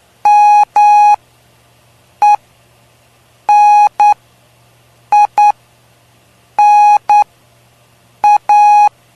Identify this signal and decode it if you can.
Morse code, heard as a keyed tone, about 9 wpm, spaced as for 6 wpm, 830 Hz. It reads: MENINA
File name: cw / py2lcd / 03-PY2LCD-Li��o 3